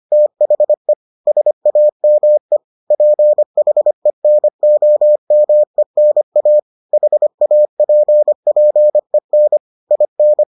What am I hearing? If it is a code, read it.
Morse code, heard as a keyed tone, about 25 wpm, 600 Hz. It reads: THE SAME PHENOMENA HAPPEN IN